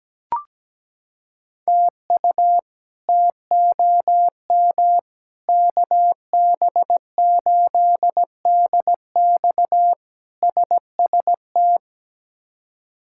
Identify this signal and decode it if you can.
Morse code, heard as a keyed tone, about 17 wpm, 700 Hz. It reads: TU TOM KB8DX SST
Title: K1USN SST QSO Practice - 17wpm